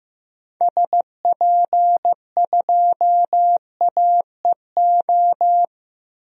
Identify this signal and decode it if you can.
Morse code, heard as a keyed tone, about 15 wpm, 700 Hz. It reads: SP2AEO
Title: DXCC-100 Call Signs - 6 Characters - 15wpm